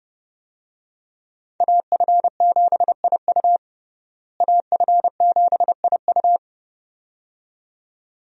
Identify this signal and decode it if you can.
Morse code, heard as a keyed tone, about 30 wpm, 700 Hz. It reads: AF7IU AF7IU
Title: USA Call Signs - 5 Characters - Repeated 2x - 30wpm